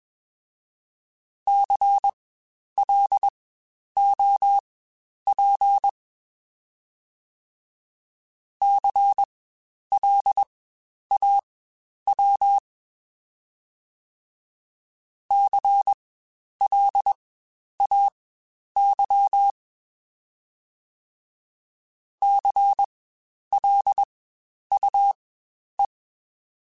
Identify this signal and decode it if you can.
Morse code, heard as a keyed tone, about 21 wpm, spaced as for 10 wpm, 800 Hz. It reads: CLOP CLAW CLAY CLUE
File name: Spaced Alphas / Spaced Alphas mp3s / Cs sp @20